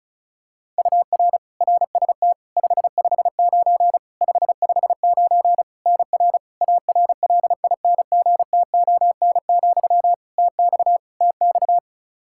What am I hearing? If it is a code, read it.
Morse code, heard as a keyed tone, about 36 wpm, 700 Hz. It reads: UR RST 559 559 NR ARLINGTON, TX TX
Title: Practice QSO - Exchange 1 - Beginning - 35wpm